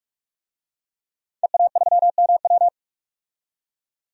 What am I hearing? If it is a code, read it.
Morse code, heard as a keyed tone, about 46 wpm, 700 Hz. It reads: EA3MW
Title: DXCC-100 Call Signs - 5 Characters - 45wpm